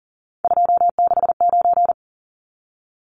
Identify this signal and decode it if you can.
Morse code, heard as a keyed tone, about 40 wpm, 700 Hz. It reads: 269
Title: Numbers - 3-Digit 40wpm